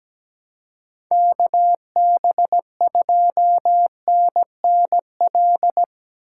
Morse code, heard as a keyed tone, about 17 wpm, 700 Hz. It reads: KB2NNL
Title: USA Call Signs - 6 Characters - 6x Character Spacing - 17wpm